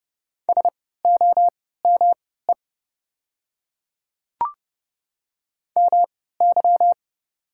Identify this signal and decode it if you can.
Morse code, heard as a keyed tone, about 30 wpm, spaced as for 17 wpm, 700 Hz. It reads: SOME MY
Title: 3x Character Spacing - Top 200 Words - 30wpm